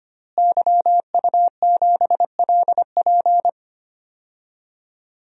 Morse code, heard as a keyed tone, about 25 wpm, 700 Hz. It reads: YU7LP
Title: DXCC-100 Call Signs - 5 Characters - Repeated 2x - 25wpm